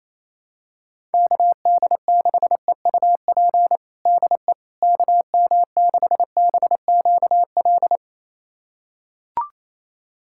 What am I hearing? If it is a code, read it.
Morse code, heard as a keyed tone, about 28 wpm, 700 Hz. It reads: KD6EUP DE KM6BQL
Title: USA Callsign DE Callsign - 6 Characters - 28wpm